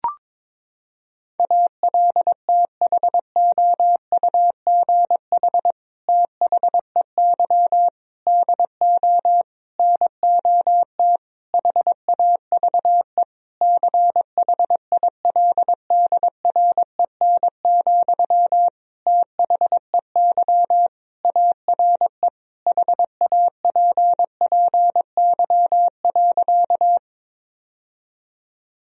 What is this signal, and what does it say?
Morse code, heard as a keyed tone, about 22 wpm, 700 Hz. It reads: ALTHOUGH THEY DO NOT HAVE CHILDREN, THEY ARE HAPPY.
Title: Sentences from Top 1000 Words - Review Part 3 - 22wpm